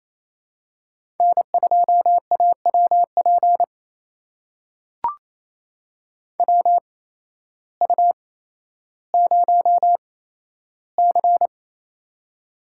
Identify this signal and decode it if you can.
Morse code, heard as a keyed tone, about 27 wpm, spaced as for 13 wpm, 700 Hz. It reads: N2AWP WU0C
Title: USA Call Signs - 8x Character Spacing - 28wpm